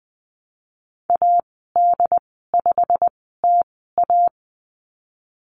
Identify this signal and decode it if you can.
Morse code, heard as a keyed tone, about 20 wpm, spaced as for 15 wpm, 700 Hz. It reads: AD5TA
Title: 088 Lesson 15g - Number 5 - Call Signs 20wpm